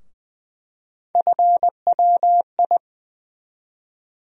Morse code, heard as a keyed tone, about 20 wpm, 700 Hz. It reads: FWI
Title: USA Call Sign Postfixes - 3 Letters 20wpm